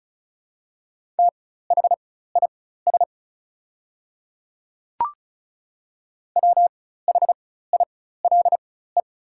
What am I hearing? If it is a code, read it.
Morse code, heard as a keyed tone, about 36 wpm, spaced as for 16 wpm, 700 Hz. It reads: THIS WHILE
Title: 4x Character Spacing - Top 300 Words - 35wpm